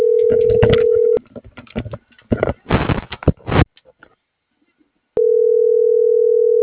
Ringback tone (North America), 440 + 480 Hz.